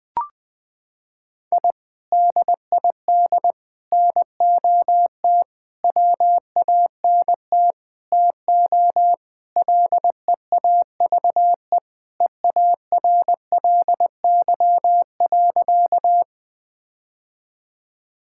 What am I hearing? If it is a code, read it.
Morse code, heard as a keyed tone, about 20 wpm, 700 Hz. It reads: I DID NOT WANT TO LEAVE EARLY.
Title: Sentences with 5-Word Phrases - 20wpm